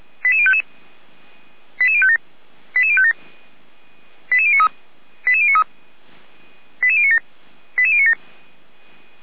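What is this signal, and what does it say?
Five-tone selective call (ZVEI), heard as quick runs of five tones, tones of about 70 ms. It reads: 88055 88057 88057 88093 88093 88097 88097
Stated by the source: ZVEI five-tone call received by SDR, from the audio server of feuerwehr-orlamuende.org (fire brigade Orlamünde).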